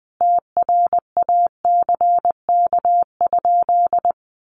Morse code, heard as a keyed tone, about 20 wpm, 700 Hz. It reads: TRACK?
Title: Sentences from Top 1100 Words 20wpm